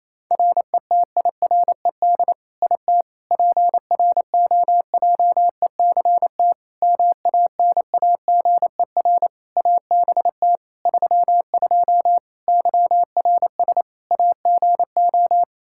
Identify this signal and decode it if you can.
Morse code, heard as a keyed tone, about 28 wpm, 700 Hz. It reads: RETIRED IT PROJECT MANAGER ABT 32 YRS AGO
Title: Practice QSO - Exchange 3 - More Info - 28wpm